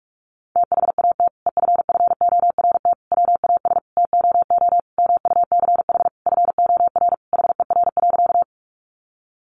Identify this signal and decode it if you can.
Morse code, heard as a keyed tone, about 44 wpm, 700 Hz. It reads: THAT EFFORT WAS TOO MUCH FOR HER.
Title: Sentences from Top 1200 Words 45wpm